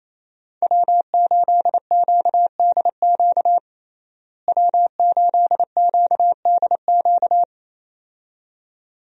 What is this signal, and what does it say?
Morse code, heard as a keyed tone, about 28 wpm, 700 Hz. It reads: W8QDQ W8QDQ
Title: USA Call Signs - 5 Characters - Repeated 2x - 28wpm